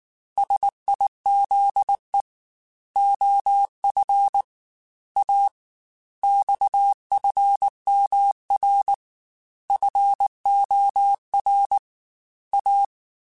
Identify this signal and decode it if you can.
Morse code, heard as a keyed tone, about 19 wpm, 800 Hz. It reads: SIZE OF A XFMR FOR A
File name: Technical Subjects / Technical Subjects-mp3s / Sw Pwr Sup @18